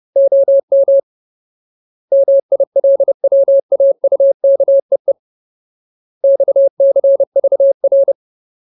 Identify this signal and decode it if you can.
Morse code, heard as a keyed tone, about 30 wpm, 550 Hz. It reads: OM MILWAUKEE XCVR
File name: Ultimate Word List / 30WPM Ultimate